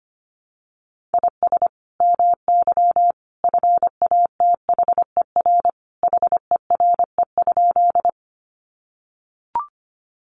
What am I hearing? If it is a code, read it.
Morse code, heard as a keyed tone, about 25 wpm, 700 Hz. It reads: IS MY FATHER HERE?
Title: Sentences from Top 300 Words 25wpm